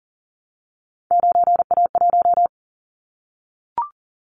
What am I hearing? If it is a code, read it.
Morse code, heard as a keyed tone, about 40 wpm, 700 Hz. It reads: 9A1
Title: Callsign Prefixes for Top 100 DXCC - 40wpm